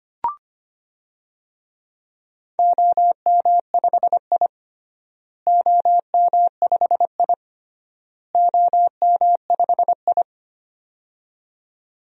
Morse code, heard as a keyed tone, about 25 wpm, 700 Hz. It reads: OM5I OM5I OM5I
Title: DXCC-100 Call Signs - 4 Characters - Repeated 3x - 25wpm